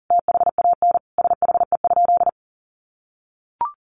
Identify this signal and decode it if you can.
Morse code, heard as a keyed tone, about 40 wpm, 700 Hz. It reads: THAN SHE?
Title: Sentences from Top 1200 Words 40wpm